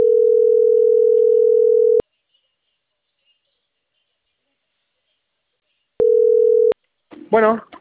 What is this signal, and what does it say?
Ringback tone (North America), 440 + 480 Hz.